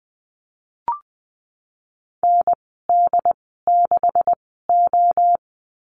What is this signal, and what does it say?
Morse code, heard as a keyed tone, about 20 wpm, spaced as for 15 wpm, 700 Hz. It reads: ND6O
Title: 154 Lesson 25g - Letter p - Call Signs 20wpm